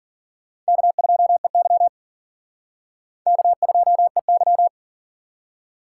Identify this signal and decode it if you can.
Morse code, heard as a keyed tone, about 50 wpm, 700 Hz. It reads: K2EY K2EY
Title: Speed Racing - USA 1x2 Call Signs 40wpm